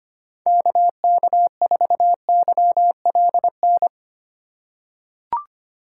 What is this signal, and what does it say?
Morse code, heard as a keyed tone, about 25 wpm, 700 Hz. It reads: KK4YLN
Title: USA Call Signs - 6 Characters - 3x Character Spacing - 25wpm